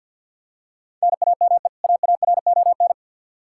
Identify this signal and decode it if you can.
Morse code, heard as a keyed tone, about 52 wpm, 700 Hz. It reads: NAME AARON
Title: Practice QSO - Exchange 1 - Beginning - 50wpm